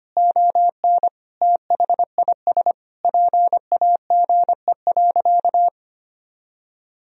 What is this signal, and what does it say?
Morse code, heard as a keyed tone, about 25 wpm, 700 Hz. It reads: ON THIS PAGE.
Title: Sentences from Top 1000 Words - Review Part 1 - 25wpm